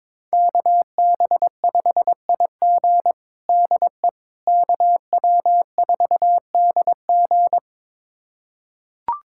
Morse code, heard as a keyed tone, about 22 wpm, 700 Hz. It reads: KB5IG DE KW4DG
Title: USA Callsign DE Callsign - 5 Characters - 22wpm